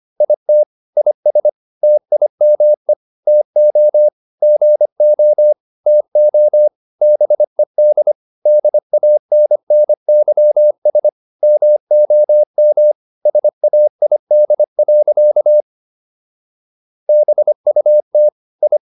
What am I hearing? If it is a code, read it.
Morse code, heard as a keyed tone, about 25 wpm, 600 Hz. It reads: IT IS TIME TO GO TO BED DANNYS MOM SAID. BUT I